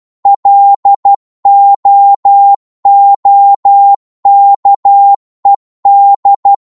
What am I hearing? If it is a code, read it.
Morse code, heard as a keyed tone, about 12 wpm, 800 Hz. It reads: LOOKED
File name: cwmp3 / psg-1-12wpm